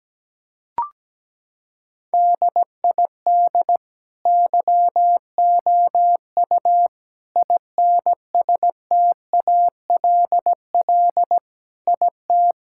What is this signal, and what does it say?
Morse code, heard as a keyed tone, about 17 wpm, 700 Hz. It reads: DID YOU INSTALL IT
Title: Sentences from Top 1600 Words 17wpm